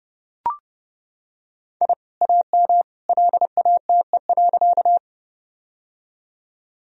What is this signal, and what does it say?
Morse code, heard as a keyed tone, about 31 wpm, 700 Hz. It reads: I AM LATE.
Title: Sentences from Top 1000 Words - Review Part 2 - 30wpm